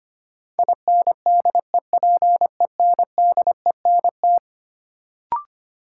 Morse code, heard as a keyed tone, about 25 wpm, 700 Hz. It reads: INDEPENDENT
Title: Sentences from Top 900 Words - New Word Drill - 25wpm